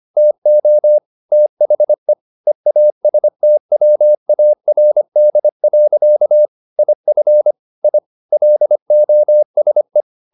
Morse code, heard as a keyed tone, about 25 wpm, 600 Hz. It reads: TO THE EASTWARD. IF I LOSE
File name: the old man and the sea part 1 25wpm